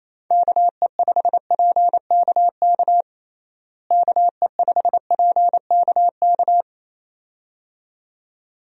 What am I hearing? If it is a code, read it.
Morse code, heard as a keyed tone, about 28 wpm, 700 Hz. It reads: KE5PKK KE5PKK
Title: USA Call Signs - 6 Characters - Repeated 2x - 28wpm